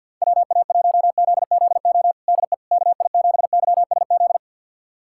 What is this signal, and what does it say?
Morse code, heard as a keyed tone, about 49 wpm, 700 Hz. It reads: WA1ZZO DE KI7YSZ
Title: USA Callsign DE Callsign - 6 Characters - 50wpm